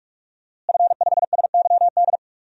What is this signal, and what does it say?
Morse code, heard as a keyed tone, about 44 wpm, 700 Hz. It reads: F5SYD
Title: DXCC-100 Call Signs - 5 Characters - Repeated 2x - 45wpm